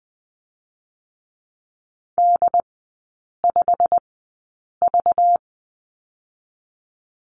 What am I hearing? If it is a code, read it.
Morse code, heard as a keyed tone, about 20 wpm, spaced as for 8 wpm, 700 Hz. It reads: D5V
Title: 191 Lesson 31e - Letter v - Three Characters 20wpm